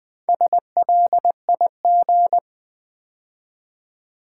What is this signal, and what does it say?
Morse code, heard as a keyed tone, about 20 wpm, 700 Hz. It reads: SLIG
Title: Build A Word - 2 Letters At A Time - 8-Letter Words - 20wpm